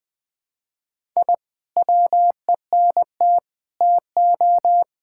Morse code, heard as a keyed tone, about 20 wpm, 700 Hz. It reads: I WENT TO
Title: Sentences from Top 1000 Words - Review Part 6 - 20wpm